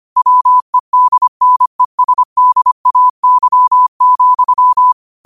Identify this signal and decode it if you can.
Morse code, heard as a keyed tone, about 25 wpm, 1000 Hz. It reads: WEDNESDAY,
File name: qotdfiles / 25wpm